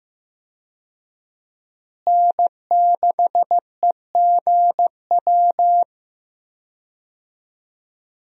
Morse code, heard as a keyed tone, about 15 wpm, 700 Hz. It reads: N6EGW
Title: USA Call Signs - 5 Characters - Repeated 3x - 15wpm